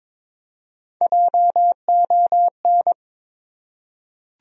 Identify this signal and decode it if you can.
Morse code, heard as a keyed tone, about 21 wpm, 700 Hz. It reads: JON
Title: USA Common Names - Max 3 Letters - 22wpm